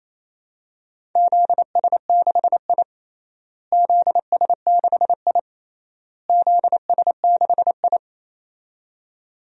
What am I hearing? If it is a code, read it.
Morse code, heard as a keyed tone, about 28 wpm, 700 Hz. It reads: ZS6I ZS6I ZS6I
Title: DXCC-100 Call Signs - 4 Characters - Repeated 3x - 28wpm